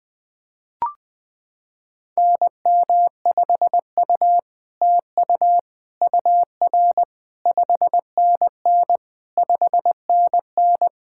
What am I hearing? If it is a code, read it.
Morse code, heard as a keyed tone, about 20 wpm, 700 Hz. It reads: NM5U TU UR 5NN 5NN
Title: SOTA QSO Practice 20wpm